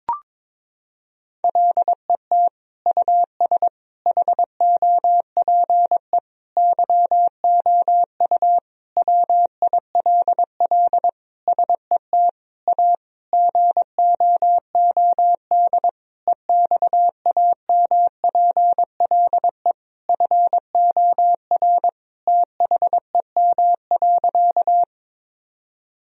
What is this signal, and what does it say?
Morse code, heard as a keyed tone, about 21 wpm, 700 Hz. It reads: LET US HOPE YOU WILL SET A GOOD EXAMPLE FOR THEM.
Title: Sentences from Top 1000 Words - Review Part 1 - 22wpm